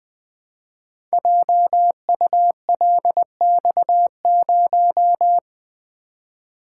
Morse code, heard as a keyed tone, about 20 wpm, 700 Hz. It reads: JULX0
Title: Code Groups - 5 Letter:Numbers - 20wpm